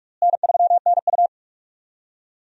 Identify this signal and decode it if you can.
Morse code, heard as a keyed tone, about 41 wpm, 700 Hz. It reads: N3NU
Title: USA Call Signs - 4 Characters - Repeated 3x - 45wpm